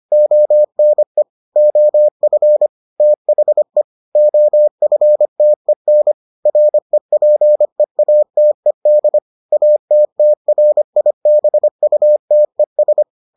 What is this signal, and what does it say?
Morse code, heard as a keyed tone, about 25 wpm, 600 Hz. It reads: ONE OF THE OFTEN REPEATED ATTRIBUTES